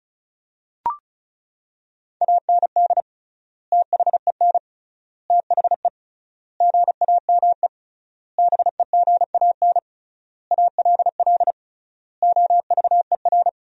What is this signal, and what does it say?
Morse code, heard as a keyed tone, about 36 wpm, 700 Hz. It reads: AND THEN THE GAME BEGAN ALL OVER
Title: 3x Word-Spacing - Sentences from Top 500 Words - 35wpm